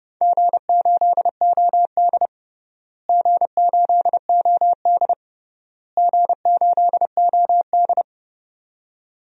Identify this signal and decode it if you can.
Morse code, heard as a keyed tone, about 30 wpm, 700 Hz. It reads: G8OD G8OD G8OD
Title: DXCC-100 Call Signs - 4 Characters - Repeated 3x - 30wpm